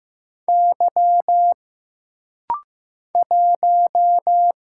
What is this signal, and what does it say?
Morse code, heard as a keyed tone, about 15 wpm, 700 Hz. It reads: Y 1
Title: Single Letter-Number Alt - Rapid-Fire 15wpm